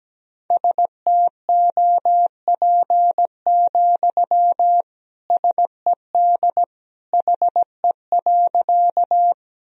Morse code, heard as a keyed tone, about 17 wpm, 700 Hz. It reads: STOP, SED HE.
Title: Sentences from Top 300 Words and Abbreviations 17wpm